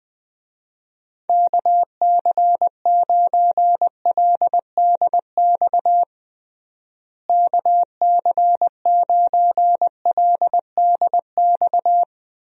Morse code, heard as a keyed tone, about 20 wpm, 700 Hz. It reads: KC9LDX KC9LDX
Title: USA Call Signs - 6 Characters - Repeated 3x - 20wpm